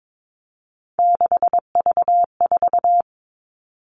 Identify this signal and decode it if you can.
Morse code, heard as a keyed tone, about 22 wpm, 700 Hz. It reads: 6V4
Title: Three Challenging Letter-Number Combinations 22wpm